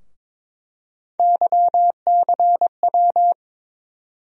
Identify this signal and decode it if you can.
Morse code, heard as a keyed tone, about 21 wpm, 700 Hz. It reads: YCW
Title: USA Call Sign Postfixes - 3 Letters 22wpm